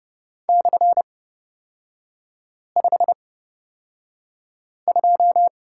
Morse code, heard as a keyed tone, about 29 wpm, spaced as for 5 wpm, 700 Hz. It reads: /52
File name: Lesson 37e - Letter x - Three Characters 30wpm